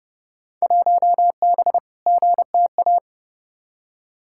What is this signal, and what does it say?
Morse code, heard as a keyed tone, about 30 wpm, 700 Hz. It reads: 1B GTA
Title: Field Day Exchanges Encore - 30wpm